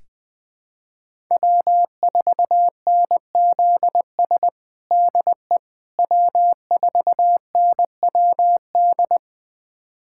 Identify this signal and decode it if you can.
Morse code, heard as a keyed tone, about 20 wpm, 700 Hz. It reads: W4NZS DE W4NWD
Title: USA Callsign DE Callsign - 5 Characters - 20wpm